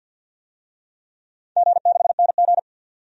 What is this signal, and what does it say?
Morse code, heard as a keyed tone, about 50 wpm, 700 Hz. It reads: G6NG
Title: DXCC-100 Call Signs - 4 Characters - 50wpm